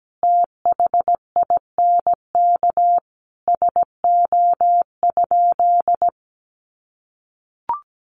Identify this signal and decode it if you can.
Morse code, heard as a keyed tone, about 17 wpm, 700 Hz. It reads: THINK SO?
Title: Sentences from Top 200 Words 17wpm